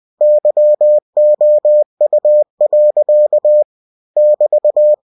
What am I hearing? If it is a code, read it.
Morse code, heard as a keyed tone, about 20 wpm, 600 Hz. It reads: YOU. <BT>
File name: cw-books / winnie-the-pooh / winnie-20wpm-page-0009